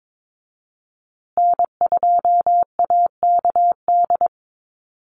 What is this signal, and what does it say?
Morse code, heard as a keyed tone, about 22 wpm, 700 Hz. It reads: N2AKD
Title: USA Call Signs Repeated 3x 22wpm